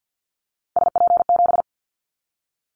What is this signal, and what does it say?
Morse code, heard as a keyed tone, about 50 wpm, 700 Hz. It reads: SP7
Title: Callsign Prefixes for Top 25 DXCC - Repeated 3x - 50wpm